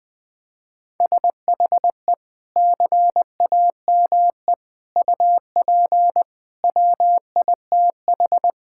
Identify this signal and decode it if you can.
Morse code, heard as a keyed tone, about 20 wpm, 700 Hz. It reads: SHE CAME UP WITH
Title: Sentences with 6-Word Phrases - 20wpm